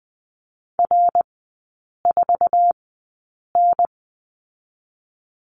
Morse code, heard as a keyed tone, about 20 wpm, spaced as for 8 wpm, 700 Hz. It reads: R4N
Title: 087 Lesson 15e - Number 5 - Three Characters 20wpm